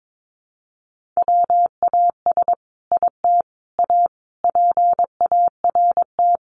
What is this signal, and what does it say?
Morse code, heard as a keyed tone, about 22 wpm, 700 Hz. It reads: WAS IT A PART
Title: Sentences from Top 1500 Words 22wpm